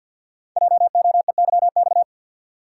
Morse code, heard as a keyed tone, about 47 wpm, 700 Hz. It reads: JOEYX
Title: Code Groups - 5 Letter:Numbers - 50wpm